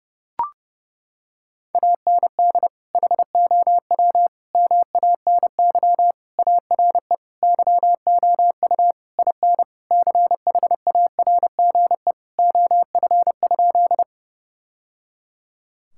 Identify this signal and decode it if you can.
Morse code, heard as a keyed tone, about 30 wpm, 700 Hz. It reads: AND HOW MANY ARE YOU IN CHARGE OF?
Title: Sentences from Top 1000 Words - Review Part 6 - 30wpm